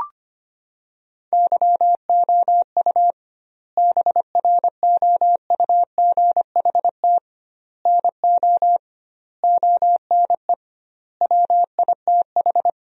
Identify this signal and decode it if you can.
Morse code, heard as a keyed tone, about 25 wpm, 700 Hz. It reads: YOU BROUGHT NO ONE WITH
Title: 2x Word-Spacing - Sentences from Top 500 Words - 25wpm